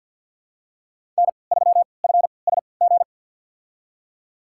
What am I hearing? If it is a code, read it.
Morse code, heard as a keyed tone, about 51 wpm, spaced as for 28 wpm, 700 Hz. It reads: N3FSG
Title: USA Call Signs - 5 Characters - 3x Character Spacing - 50wpm